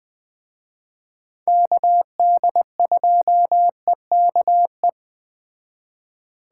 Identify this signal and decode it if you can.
Morse code, heard as a keyed tone, about 21 wpm, 700 Hz. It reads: KD2EKE
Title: Speed Racing - USA 2x3 Call Signs 17wpm